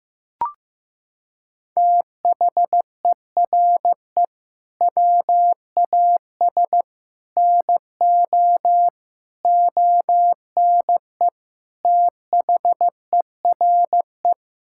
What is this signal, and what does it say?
Morse code, heard as a keyed tone, about 15 wpm, 700 Hz. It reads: THERE WAS NO ONE THERE
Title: Sentences from Top 1600 Words 15wpm